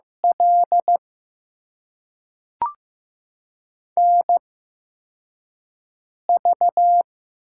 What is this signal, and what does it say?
Morse code, heard as a keyed tone, about 15 wpm, spaced as for 4 wpm, 700 Hz. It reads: L NV
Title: USA Call Signs - 8x Character Spacing - 15wpm